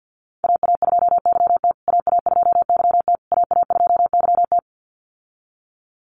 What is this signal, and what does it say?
Morse code, heard as a keyed tone, about 50 wpm, 700 Hz. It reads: AA2YT AA2YT AA2YT
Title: USA Call Signs Repeated 3x 50wpm